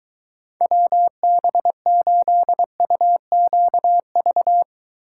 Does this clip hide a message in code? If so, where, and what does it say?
Yes: Morse code from 0.6 to 4.6 s, heard as a keyed tone, about 23 wpm, 700 Hz. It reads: WB8UQV